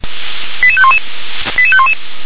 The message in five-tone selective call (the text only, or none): 88511 88511